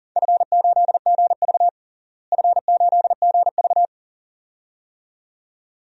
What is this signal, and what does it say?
Morse code, heard as a keyed tone, about 40 wpm, 700 Hz. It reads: F8GV F8GV
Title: DXCC-100 Call Signs - 4 Characters - Repeated 2x - 40wpm